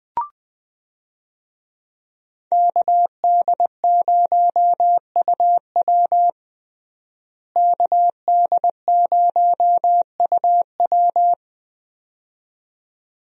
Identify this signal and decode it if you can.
Morse code, heard as a keyed tone, about 20 wpm, 700 Hz. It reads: KD0UW KD0UW
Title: USA Call Signs - 5 Characters - Repeated 2x - 20wpm